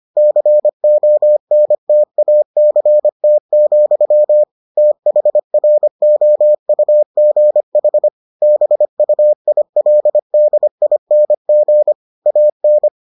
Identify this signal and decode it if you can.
Morse code, heard as a keyed tone, about 25 wpm, 600 Hz. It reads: CONTACT, THROUGH BUILDING AN